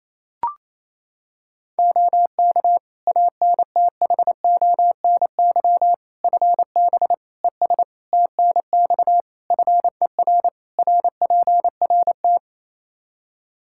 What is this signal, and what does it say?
Morse code, heard as a keyed tone, about 28 wpm, 700 Hz. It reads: OK ANTHONY FB ES TNX FER RPRT
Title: Practice QSO - Exchange 2 - Info - 28wpm